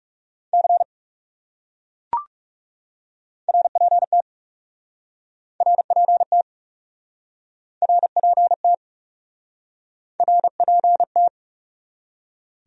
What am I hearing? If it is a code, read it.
Morse code, heard as a keyed tone, about 40 wpm, 700 Hz. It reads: C RPT RPT RPT RPT